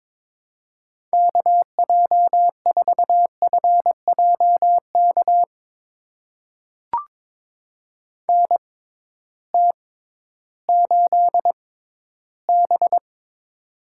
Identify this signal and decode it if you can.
Morse code, heard as a keyed tone, about 22 wpm, spaced as for 13 wpm, 700 Hz. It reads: KJ4FJK NT8B
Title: USA Call Signs - 6x Character Spacing - 22wpm